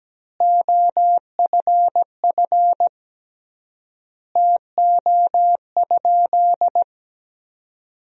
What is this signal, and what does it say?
Morse code, heard as a keyed tone, about 17 wpm, 700 Hz. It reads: OFF TO?
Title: 3x Word-Spacing - Sentences from Top 200 Words - 17wpm